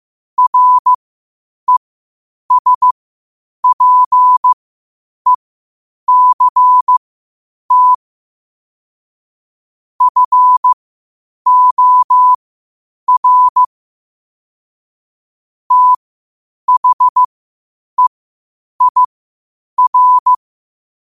Morse code, heard as a keyed tone, about 15 wpm, spaced as for 8 wpm, 1000 Hz. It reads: RESPECT FOR THEIR
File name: qotdfiles / 8wpm